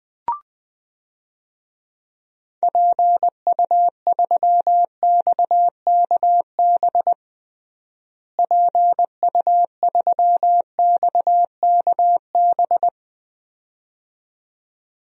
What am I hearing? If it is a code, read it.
Morse code, heard as a keyed tone, about 20 wpm, 700 Hz. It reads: PU3XKB PU3XKB
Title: DXCC-100 Call Signs - 6 Characters - Repeated 2x - 20wpm